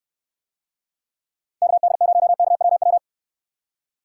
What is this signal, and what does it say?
Morse code, heard as a keyed tone, about 64 wpm, 700 Hz. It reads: KD0KWW